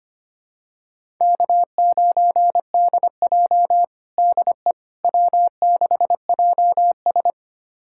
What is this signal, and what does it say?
Morse code, heard as a keyed tone, about 25 wpm, 700 Hz. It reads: K9DJ DE W6JS
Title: USA Callsign DE Callsign - 4 Characters - 25wpm